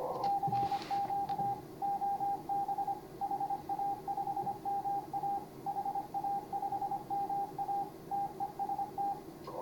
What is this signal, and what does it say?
Morse code, heard as a keyed tone, about 25 wpm, 810 Hz, with radio noise. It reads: CQ OB HA5MA HA5MA TEST
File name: CW / HA5MA RHOB HANG CW 2017